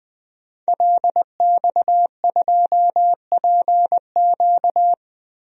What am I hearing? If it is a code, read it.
Morse code, heard as a keyed tone, about 20 wpm, 700 Hz. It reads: LX2PQ
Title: DXCC-100 Call Signs - 5 Characters - 20wpm